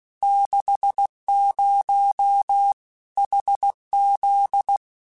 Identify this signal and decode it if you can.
Morse code, heard as a keyed tone, about 16 wpm, 800 Hz. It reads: 60 HZ
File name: Technical Subjects / Technical Subjects-mp3s / Sw Pwr Sup @15